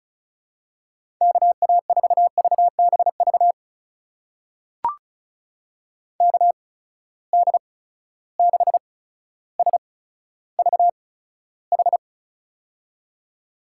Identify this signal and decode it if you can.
Morse code, heard as a keyed tone, about 34 wpm, spaced as for 15 wpm, 700 Hz. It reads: KA4VBV KD6SVH
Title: USA Call Signs - 6 Characters - 8x Character Spacing - 35wpm